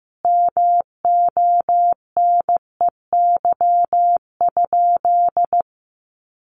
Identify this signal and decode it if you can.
Morse code, heard as a keyed tone, about 15 wpm, 700 Hz. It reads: MONEY?
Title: Sentences from Top 1400 Words 15wpm